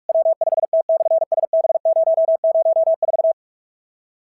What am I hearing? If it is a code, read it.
Morse code, heard as a keyed tone, about 44 wpm, 650 Hz. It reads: W5T/SB004
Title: SOTA QSO Practice 45wpm